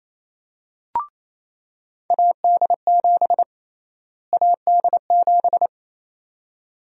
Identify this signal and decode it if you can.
Morse code, heard as a keyed tone, about 28 wpm, 700 Hz. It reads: AD7 AD7
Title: USA Call Sign Prefixes - Repeated 2x - 28wpm